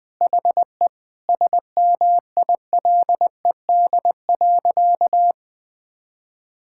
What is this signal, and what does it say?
Morse code, heard as a keyed tone, about 20 wpm, 700 Hz. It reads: HE SMILED.
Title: Sentences with 5-Word Phrases - 20wpm